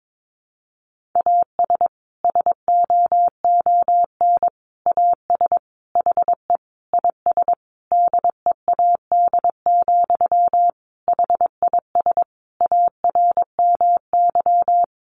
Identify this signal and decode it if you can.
Morse code, heard as a keyed tone, about 22 wpm, 700 Hz. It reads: AS SOON AS HE IS DEAD, HIS ARMY
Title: Sentences from Top 1500 Words 22wpm